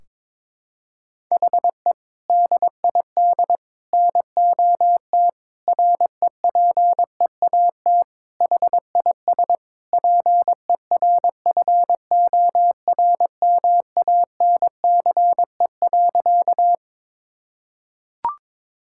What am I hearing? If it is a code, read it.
Morse code, heard as a keyed tone, about 22 wpm, 700 Hz. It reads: HE DID NOT REPEAT HIS PERFORMANCE.
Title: Sentences from Top 1600 Words 22wpm